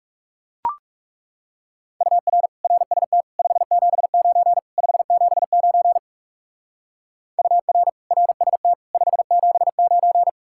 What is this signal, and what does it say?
Morse code, heard as a keyed tone, about 36 wpm, 700 Hz. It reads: UR RST 579 579 UR RST 579
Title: Speed-Racing Common QSO Phrases - 30wpm